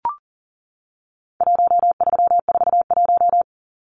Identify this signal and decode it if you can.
Morse code, heard as a keyed tone, about 40 wpm, 700 Hz. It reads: 1341
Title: Numbers - 4-Digit 40wpm